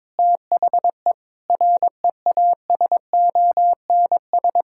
Morse code, heard as a keyed tone, about 22 wpm, 700 Hz. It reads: THE REASONS
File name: Sentences with 7-Word Phrases - 22wpm